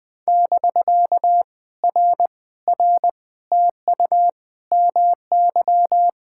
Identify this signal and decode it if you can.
Morse code, heard as a keyed tone, about 20 wpm, 700 Hz. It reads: <BK> R R TU MY
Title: POTA QSO Practice - 20wpm